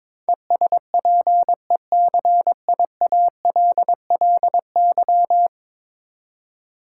Morse code, heard as a keyed tone, about 22 wpm, 700 Hz. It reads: ESPECIALLY
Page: morse-code-ninja-practice-22wpm